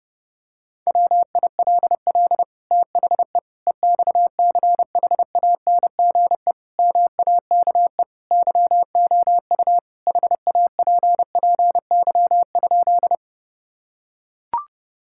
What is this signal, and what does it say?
Morse code, heard as a keyed tone, about 32 wpm, 700 Hz. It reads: WILL THE EXCHANGE MAKE YOU HAPPY?
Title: Sentences from Top 1000 Words - Review Part 6 - 30wpm